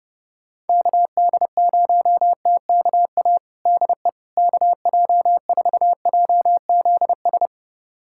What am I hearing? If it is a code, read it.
Morse code, heard as a keyed tone, about 30 wpm, 700 Hz. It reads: KD0TKA DE KJ4JZS
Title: USA Callsign DE Callsign - 6 Characters - 30wpm